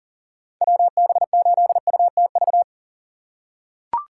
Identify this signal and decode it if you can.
Morse code, heard as a keyed tone, about 40 wpm, 700 Hz. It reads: WB8UTV
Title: USA Call Signs - 6 Characters - 4x Character Spacing - 40wpm